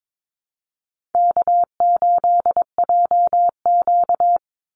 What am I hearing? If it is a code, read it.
Morse code, heard as a keyed tone, about 22 wpm, 700 Hz. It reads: K8JQ
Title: Easy Call Signs Repeated 3x - 22wpm